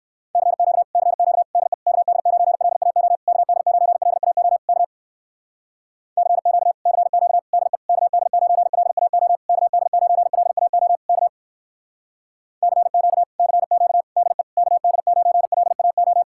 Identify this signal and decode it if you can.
Morse code, heard as a keyed tone, about 57 wpm, 700 Hz. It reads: CQ CQ DE KD9LAO KD9LAO K CQ CQ DE KD9LAO KD9LAO K CQ CQ DE KD9LAO